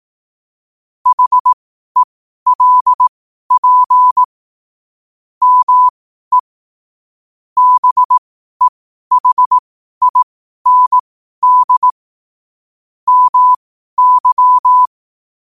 Morse code, heard as a keyed tone, about 18 wpm, spaced as for 12 wpm, 1000 Hz. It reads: HELP ME BEHIND MY
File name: qotdfiles / 12wpm